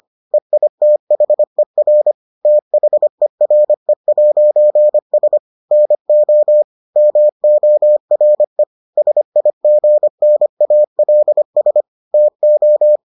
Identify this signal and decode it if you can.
Morse code, heard as a keyed tone, about 25 wpm, 600 Hz. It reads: EITHER THERE'S NO MORE SIGNALS TO